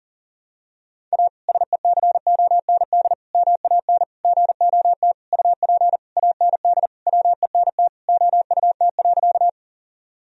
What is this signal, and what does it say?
Morse code, heard as a keyed tone, about 40 wpm, 700 Hz. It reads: A SECOND MAN GOT UP AND WENT OUT.
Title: Sentences from Top 1000 Words - Review Part 1 - 40wpm